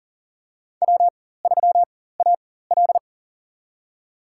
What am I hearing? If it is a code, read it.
Morse code, heard as a keyed tone, about 40 wpm, spaced as for 19 wpm, 700 Hz. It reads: W3AL
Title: USA Call Signs - 4 Characters - 4x Character Spacing - 40wpm